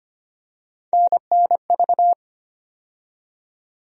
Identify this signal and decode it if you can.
Morse code, heard as a keyed tone, about 25 wpm, 700 Hz. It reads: NNV
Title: USA Call Sign Postfixes - 3 Letters 25wpm